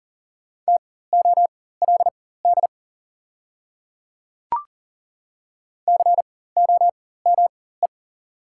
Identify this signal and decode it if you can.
Morse code, heard as a keyed tone, about 40 wpm, spaced as for 19 wpm, 700 Hz. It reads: TOLD COME
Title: 4x Character Spacing - Top 500 Words - 40wpm